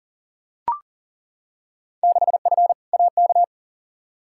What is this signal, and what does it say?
Morse code, heard as a keyed tone, about 40 wpm, 700 Hz. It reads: 6F AK
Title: Field Day Exchanges Encore - 40wpm